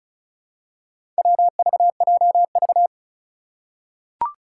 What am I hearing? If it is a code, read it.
Morse code, heard as a keyed tone, about 36 wpm, 700 Hz. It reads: WVJV
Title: Four Challenging Letter-Number Combinations 35wpm